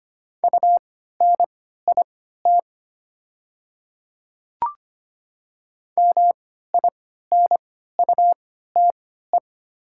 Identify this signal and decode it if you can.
Morse code, heard as a keyed tone, about 25 wpm, spaced as for 14 wpm, 700 Hz. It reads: UNIT MINUTE
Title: 3x Character Spacing - Top 500 Words - 25wpm